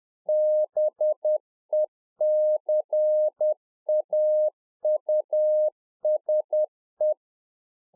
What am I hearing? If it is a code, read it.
Morse code, heard as a keyed tone, about 10 wpm, 610 Hz. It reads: BECAUSE
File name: 2011-04-mp3 / A SMALL TRANCEIVER 1 10WPM